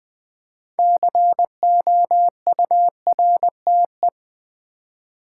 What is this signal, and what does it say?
Morse code, heard as a keyed tone, about 20 wpm, 700 Hz. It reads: COURTE
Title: Build A Word - 2 Letters At A Time - 8-Letter Words - 20wpm